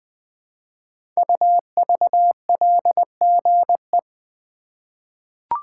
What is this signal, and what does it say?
Morse code, heard as a keyed tone, about 20 wpm, 700 Hz. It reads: UVLGE
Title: Code Groups - 5 Letter:Numbers - 20wpm